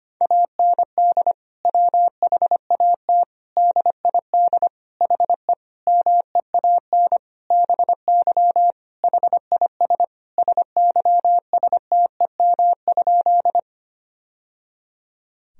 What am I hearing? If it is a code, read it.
Morse code, heard as a keyed tone, about 25 wpm, 700 Hz. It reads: AND WHAT DID HE MEAN BY HIS SYSTEM?
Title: Sentences from Top 1000 Words - Review Part 3 - 25wpm